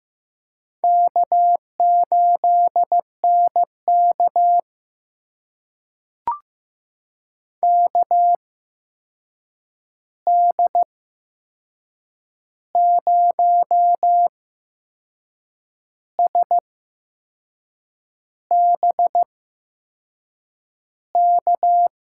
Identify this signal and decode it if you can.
Morse code, heard as a keyed tone, about 15 wpm, spaced as for 6 wpm, 700 Hz. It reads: K8NK KD0SBK
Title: USA Call Signs - 8x Character Spacing - 15wpm